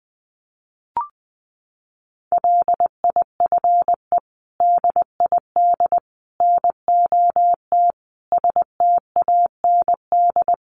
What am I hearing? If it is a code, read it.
Morse code, heard as a keyed tone, about 20 wpm, 700 Hz. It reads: LIFE DID NOT STAND
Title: Sentences from Top 800 Words 20wpm